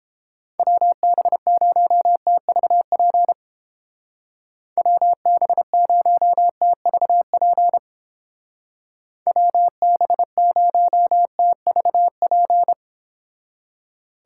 Morse code, heard as a keyed tone, about 31 wpm, 700 Hz. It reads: WB0TVP WB0TVP WB0TVP